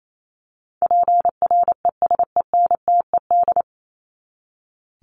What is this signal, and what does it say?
Morse code, heard as a keyed tone, about 28 wpm, 700 Hz. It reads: PRESENTED
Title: ED Suffix - Top 100 Words with Base Words 28wpm